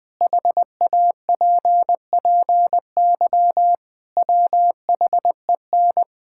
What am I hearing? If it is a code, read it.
Morse code, heard as a keyed tone, about 20 wpm, 700 Hz. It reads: HAPPY WHEN
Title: Sentences with 2-Word Phrases - Spoken Beginning - 20wpm